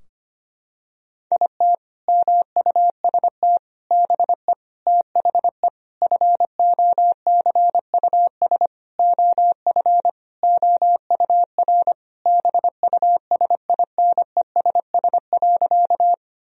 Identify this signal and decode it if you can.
Morse code, heard as a keyed tone, about 25 wpm, 700 Hz. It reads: IT MUST BE THE FOCUS OF OUR BUSINESS.
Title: Sentences from Top 1000 Words - Review Part 4 - 25wpm